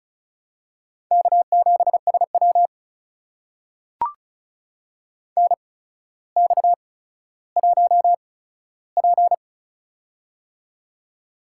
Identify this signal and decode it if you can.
Morse code, heard as a keyed tone, about 36 wpm, spaced as for 15 wpm, 700 Hz. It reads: K7SW NX1P
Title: USA Call Signs - 4 Characters - 8x Character Spacing - 35wpm